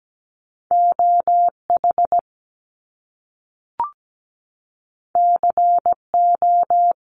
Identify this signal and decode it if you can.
Morse code, heard as a keyed tone, about 17 wpm, 700 Hz. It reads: OH CO
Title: US States 17wpm